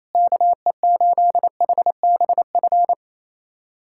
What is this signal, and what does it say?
Morse code, heard as a keyed tone, about 28 wpm, 700 Hz. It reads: KE8HBF
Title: USA Callsign DE Callsign - 6 Characters - 28wpm